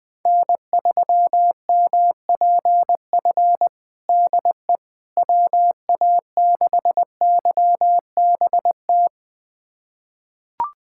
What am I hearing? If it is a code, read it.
Morse code, heard as a keyed tone, about 21 wpm, 700 Hz. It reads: N3MPF DE WA6YBT
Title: USA Callsign DE Callsign 20wpm